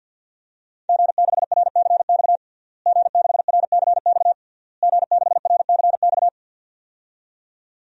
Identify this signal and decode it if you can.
Morse code, heard as a keyed tone, about 52 wpm, 700 Hz. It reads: G6RCX G6RCX G6RCX
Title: DXCC-100 Call Signs - 5 Characters - Repeated 3x - 50wpm